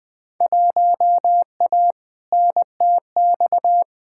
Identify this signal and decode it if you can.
Morse code, heard as a keyed tone, about 20 wpm, 700 Hz. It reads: 1A NTX
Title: Field Day Exchanges - 20wpm